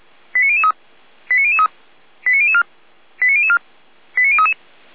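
Five-tone selective call (ZVEI), heard as quick runs of five tones, tones of about 70 ms. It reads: 89003 89003 89004 89004 89033
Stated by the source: ZVEI five-tone selective call 89001 received on an SDR, from the Feuerwehr Orlamünde audio archive.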